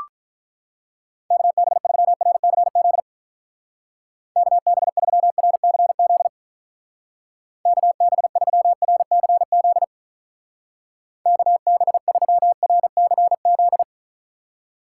Morse code, heard as a keyed tone, about 43 wpm, 700 Hz. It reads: KB3RCZ KB3RCZ KB3RCZ KB3RCZ
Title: Speed Racing - USA 2x3 Call Signs 35wpm